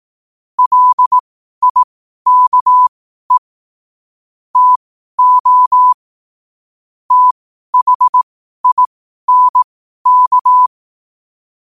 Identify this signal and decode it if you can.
Morse code, heard as a keyed tone, about 18 wpm, spaced as for 12 wpm, 1000 Hz. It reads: LIKE TO THINK
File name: qotdfiles / 12wpm